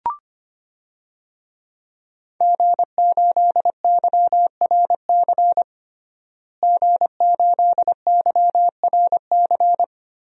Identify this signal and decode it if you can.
Morse code, heard as a keyed tone, about 25 wpm, 700 Hz. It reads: G8YRC G8YRC
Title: DXCC-100 Call Signs - 5 Characters - Repeated 2x - 25wpm